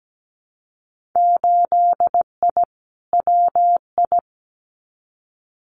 Morse code, heard as a keyed tone, about 17 wpm, 700 Hz. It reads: 8I WI
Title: Winter Field Day Exchanges 17wpm